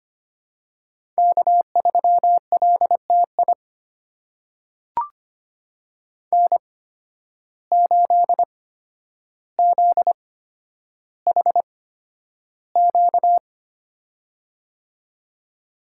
Morse code, heard as a keyed tone, about 25 wpm, spaced as for 11 wpm, 700 Hz. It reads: K3LTI N8ZHQ
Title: USA Call Signs - 5 Characters - 8x Character Spacing - 25wpm